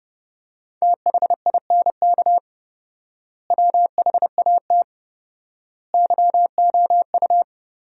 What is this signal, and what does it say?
Morse code, heard as a keyed tone, about 30 wpm, 700 Hz. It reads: THINK WHAT YOU
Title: 4x Word-Spacing - Sentences from Top 200 Words - 30wpm